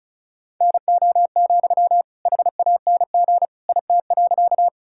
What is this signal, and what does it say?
Morse code, heard as a keyed tone, about 34 wpm, 700 Hz. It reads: NO, HANG IT.
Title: Sentences from Top 1600 Words 35wpm